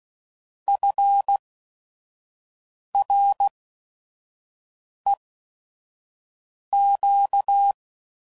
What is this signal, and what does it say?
Morse code, heard as a keyed tone, about 16 wpm, 800 Hz. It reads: F R E Q